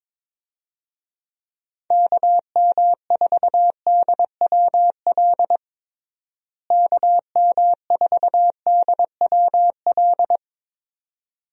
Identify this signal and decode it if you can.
Morse code, heard as a keyed tone, about 22 wpm, 700 Hz. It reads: KM4DWL KM4DWL
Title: USA Call Signs - 6 Characters - Repeated 2x - 22wpm